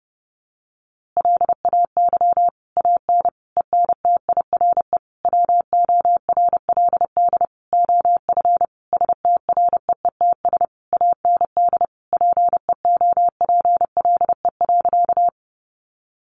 Morse code, heard as a keyed tone, about 30 wpm, 700 Hz. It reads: LAY AN ENTIRE WORLD OF STREETS AND PEOPLE.
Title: A Sound Of Thunder 30wpm